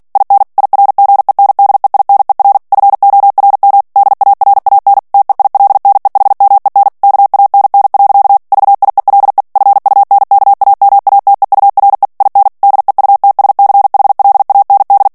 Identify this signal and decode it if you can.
Morse code, heard as a keyed tone, about 48 wpm, 800 Hz. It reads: IN IRGENDEINER FORM DARAN TEILNEHMEN KANN. VIELE FUNKAMATEURE IN DEUTSCHLAND
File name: morse mp3 / ALLEMAND texte clair 2 vitesse 45 mots minute